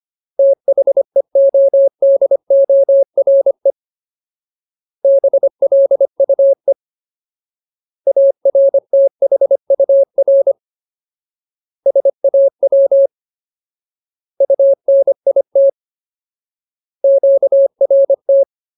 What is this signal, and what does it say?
Morse code, heard as a keyed tone, about 25 wpm, 550 Hz. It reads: THEODORE BLUE ARTHUR SAW UNIT QRT
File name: Ultimate Word List / 25WPM Ultimate